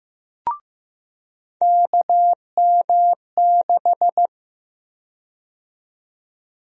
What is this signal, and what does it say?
Morse code, heard as a keyed tone, about 15 wpm, 700 Hz. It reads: KM6
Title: USA Call Sign Prefixes - 15wpm